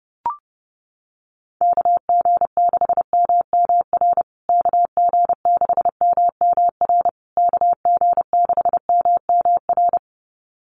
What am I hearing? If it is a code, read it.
Morse code, heard as a keyed tone, about 30 wpm, 700 Hz. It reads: KG6MMR KG6MMR KG6MMR
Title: USA Call Signs Repeated 3x 30wpm